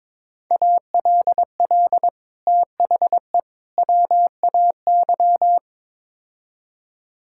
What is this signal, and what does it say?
Morse code, heard as a keyed tone, about 22 wpm, 700 Hz. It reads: ALL THE WAY
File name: practice / 22wpm / Build A Phrase - 5-Word Phrases - 22wpm